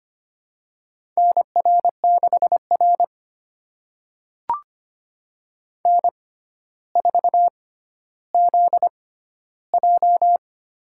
Morse code, heard as a keyed tone, about 25 wpm, spaced as for 13 wpm, 700 Hz. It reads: NR6R N4ZJ
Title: USA Call Signs - 4 Characters - 6x Character Spacing - 25wpm